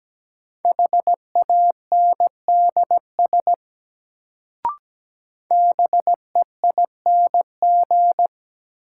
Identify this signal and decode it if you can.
Morse code, heard as a keyed tone, about 17 wpm, 700 Hz. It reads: HANDS BEING
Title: Sets of 4 Words - New Word Drill - 17wpm